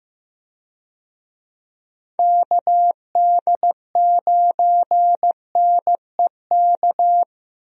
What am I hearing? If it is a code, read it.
Morse code, heard as a keyed tone, about 15 wpm, 700 Hz. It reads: KD9NEK
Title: USA Call Signs - 6 Characters - Repeated 3x - 15wpm